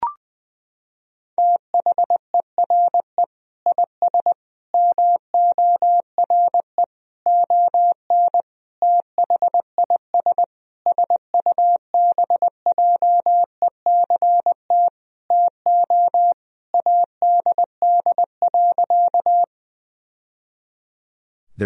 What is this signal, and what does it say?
Morse code, heard as a keyed tone, about 20 wpm, 700 Hz. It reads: THERE IS MORE ON THIS SUBJECT TO ADD.
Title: Sentences from Top 1000 Words - Review Part 1 - 20wpm